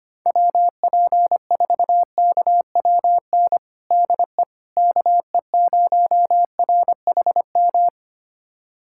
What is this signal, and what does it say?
Morse code, heard as a keyed tone, about 25 wpm, 700 Hz. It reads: WP4KWN DE KE0RHM
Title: USA Callsign DE Callsign - 6 Characters - 25wpm